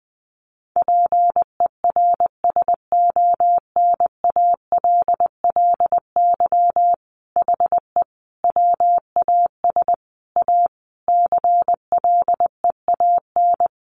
Morse code, heard as a keyed tone, about 20 wpm, 700 Hz. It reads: PERSONALLY HE WAS A CLEAN
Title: Sentences from Top 1400 Words 20wpm